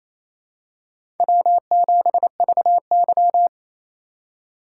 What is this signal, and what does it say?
Morse code, heard as a keyed tone, about 28 wpm, 700 Hz. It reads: W7VY
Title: USA Call Signs - 3x Character Spacing - 28wpm